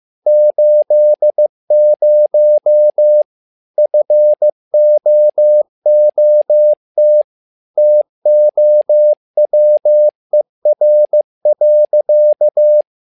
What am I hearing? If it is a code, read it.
Morse code, heard as a keyed tone, about 15 wpm, 600 Hz. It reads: 80 FOOT TOWER.
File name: learning center / tower trouble / tower trouble 15wpm